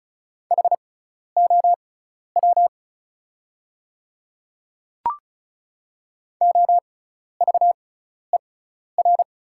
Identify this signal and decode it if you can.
Morse code, heard as a keyed tone, about 34 wpm, spaced as for 12 wpm, 700 Hz. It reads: HOW OVER
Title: 6x Character Spacing - Top 100 Words - 35wpm